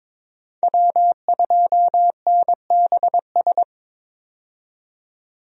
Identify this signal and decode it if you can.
Morse code, heard as a keyed tone, about 22 wpm, 700 Hz. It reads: W2NBS
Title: USA Call Signs - 5 Characters - 22wpm